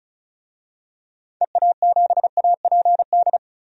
Morse code, heard as a keyed tone, about 35 wpm, 700 Hz. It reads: EA7APD
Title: DXCC-100 Call Signs - 6 Characters - 35wpm